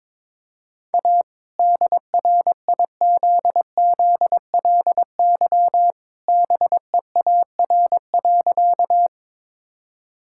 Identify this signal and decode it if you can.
Morse code, heard as a keyed tone, about 22 wpm, 700 Hz. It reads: A DRIZZLY BEAR.
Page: morse-code-ninja-practice-22wpm